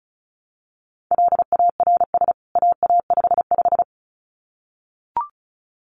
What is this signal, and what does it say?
Morse code, heard as a keyed tone, about 34 wpm, 700 Hz. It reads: LARS AA55
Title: CWT Exchange 35wpm